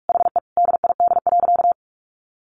Morse code, heard as a keyed tone, about 44 wpm, 700 Hz. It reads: HE DID.
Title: Sentences from Top 1200 Words 45wpm